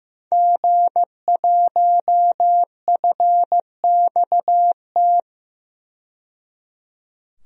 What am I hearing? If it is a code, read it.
Morse code, heard as a keyed tone, about 15 wpm, 700 Hz. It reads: G1FXT
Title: DXCC-100 Call Signs - 5 Characters - 15wpm